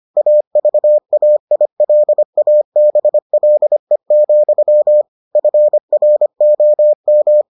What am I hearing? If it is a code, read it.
Morse code, heard as a keyed tone, about 25 wpm, 600 Hz. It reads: AVAILABLE, FROM